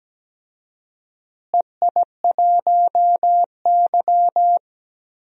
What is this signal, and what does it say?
Morse code, heard as a keyed tone, about 17 wpm, 700 Hz. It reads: EI1Y
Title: DXCC-100 Call Signs - 4 Characters - Repeated 2x - 17wpm